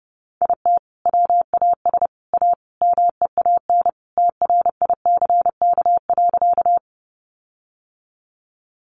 Morse code, heard as a keyed tone, about 30 wpm, 700 Hz. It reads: IT WAS A MEAN TRICK.
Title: Sentences from Top 1500 Words 30wpm